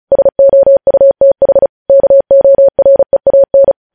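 Morse code, heard as a keyed tone, about 36 wpm, 550 Hz. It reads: SOUTH KOREAN
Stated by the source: News headlines sent in Morse code at 35 WPM.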